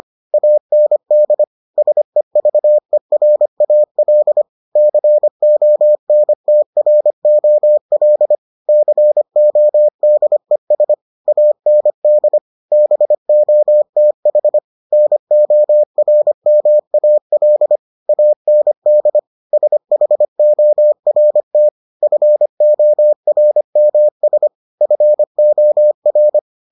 Morse code, heard as a keyed tone, about 25 wpm, 600 Hz. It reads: AND SEVERAL CONTROL CODES AND BOTH NORMAL AND SHORT FORMS FOR